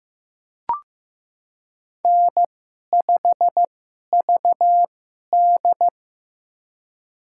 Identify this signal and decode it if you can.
Morse code, heard as a keyed tone, about 15 wpm, spaced as for 11 wpm, 700 Hz. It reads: N5VD
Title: USA Call Signs - 2x Character Spacing - 15wpm